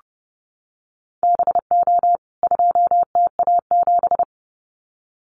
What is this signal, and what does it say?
Morse code, heard as a keyed tone, about 30 wpm, 700 Hz. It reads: BO 2TA7
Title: CWT Exchange 30wpm